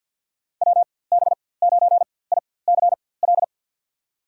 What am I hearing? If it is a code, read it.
Morse code, heard as a keyed tone, about 48 wpm, spaced as for 23 wpm, 700 Hz. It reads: WB9ICL
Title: USA Call Signs - 6 Characters - 4x Character Spacing - 50wpm